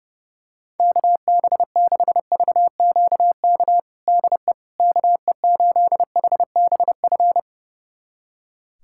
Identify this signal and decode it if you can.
Morse code, heard as a keyed tone, about 30 wpm, 700 Hz. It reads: KB6VQK DE KE8HBF
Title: USA Callsign DE Callsign - 6 Characters - 30wpm